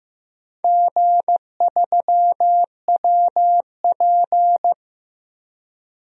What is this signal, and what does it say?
Morse code, heard as a keyed tone, about 15 wpm, 700 Hz. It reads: G3WP
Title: DXCC-100 Call Signs - 4 Characters - 15wpm